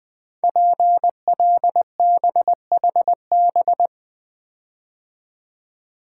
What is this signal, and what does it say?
Morse code, heard as a keyed tone, about 20 wpm, 700 Hz. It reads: PLBHB
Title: Code Groups - 5 Letter:Numbers - 20wpm